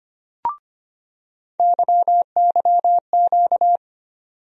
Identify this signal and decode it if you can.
Morse code, heard as a keyed tone, about 25 wpm, 700 Hz. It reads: YYQ
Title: USA Call Sign Postfixes - 3 Letters 25wpm